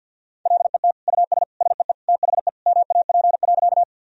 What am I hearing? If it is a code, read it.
Morse code, heard as a keyed tone, about 52 wpm, 700 Hz. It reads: LET US SEE THE MAP.
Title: Sentences from Top 1000 Words - Review Part 2 - 50wpm